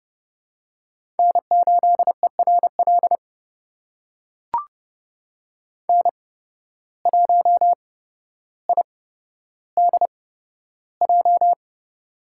Morse code, heard as a keyed tone, about 30 wpm, spaced as for 13 wpm, 700 Hz. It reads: N8ERL N1IDJ
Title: USA Call Signs - 8x Character Spacing - 30wpm